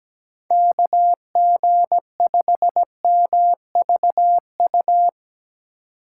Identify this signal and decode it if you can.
Morse code, heard as a keyed tone, about 17 wpm, 700 Hz. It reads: KG5MVU
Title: USA Call Signs - 6 Characters - 2x Character Spacing - 17wpm